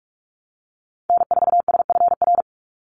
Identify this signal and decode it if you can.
Morse code, heard as a keyed tone, about 44 wpm, 700 Hz. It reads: N4SFR
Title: NA SOTA Activator Callsigns 45wpm